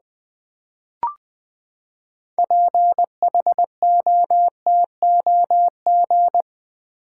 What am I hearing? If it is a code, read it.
Morse code, heard as a keyed tone, about 20 wpm, 700 Hz. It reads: PHOTOG
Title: Build A Word - 6 Letters At A Time - 12-Letter Words - 20wpm